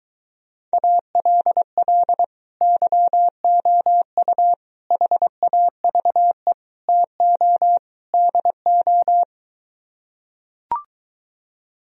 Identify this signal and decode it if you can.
Morse code, heard as a keyed tone, about 24 wpm, 700 Hz. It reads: ALL YOU HAVE TO DO